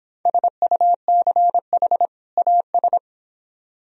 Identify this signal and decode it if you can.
Morse code, heard as a keyed tone, about 25 wpm, 700 Hz. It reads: SUCH AS